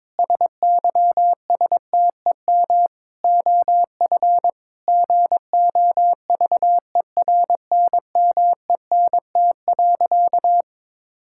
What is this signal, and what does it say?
Morse code, heard as a keyed tone, about 22 wpm, 700 Hz. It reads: SYSTEM OF GOVERNMENT.
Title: Sentences from Top 1000 Words - Review Part 5 - 22wpm